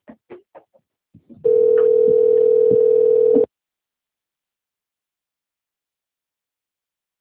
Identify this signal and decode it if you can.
Ringback tone (North America), 440 + 480 Hz.